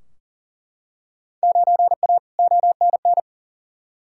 Morse code, heard as a keyed tone, about 40 wpm, 700 Hz. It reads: 9A ONN
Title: Field Day Exchanges Encore - 40wpm